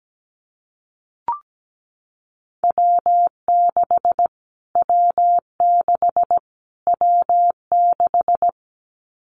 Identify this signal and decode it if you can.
Morse code, heard as a keyed tone, about 17 wpm, 700 Hz. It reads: W6 W6 W6
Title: Callsign Prefixes for Top 25 DXCC - Repeated 3x - 17wpm